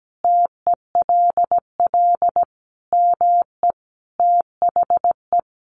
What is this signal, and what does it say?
Morse code, heard as a keyed tone, about 17 wpm, 700 Hz. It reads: TELL ME THE
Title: Sentences from Top 1400 Words 17wpm